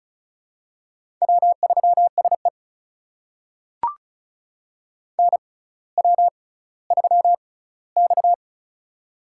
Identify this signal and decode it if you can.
Morse code, heard as a keyed tone, about 36 wpm, spaced as for 18 wpm, 700 Hz. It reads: W3SE NW3X
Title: USA Call Signs - 4 Characters - 6x Character Spacing - 35wpm